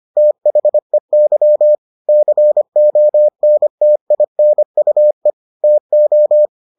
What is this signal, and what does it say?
Morse code, heard as a keyed tone, about 25 wpm, 600 Hz. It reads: THEY CONTINUE TO